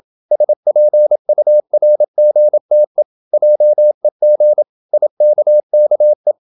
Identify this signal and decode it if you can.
Morse code, heard as a keyed tone, about 28 wpm, 600 Hz. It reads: SPURGTE JEG IKKE
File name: Ferie QRQ 27wpm